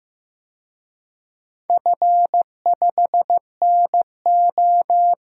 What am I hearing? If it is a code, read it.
Morse code, heard as a keyed tone, about 15 wpm, 700 Hz. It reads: F5NO
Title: DXCC-100 Call Signs - 4 Characters - Repeated 2x - 15wpm